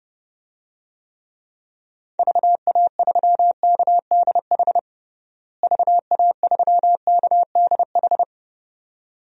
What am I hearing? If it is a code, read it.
Morse code, heard as a keyed tone, about 30 wpm, 700 Hz. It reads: VA3KDH VA3KDH
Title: DXCC-100 Call Signs - 6 Characters - Repeated 2x - 30wpm